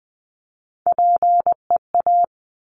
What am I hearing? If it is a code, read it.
Morse code, heard as a keyed tone, about 19 wpm, 700 Hz. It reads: PEA
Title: Top Three Letter Combinations in Words 20wpm